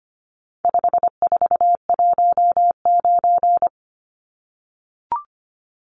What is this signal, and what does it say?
Morse code, heard as a keyed tone, about 25 wpm, 700 Hz. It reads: 5419
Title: Numbers - 4-Digit 25wpm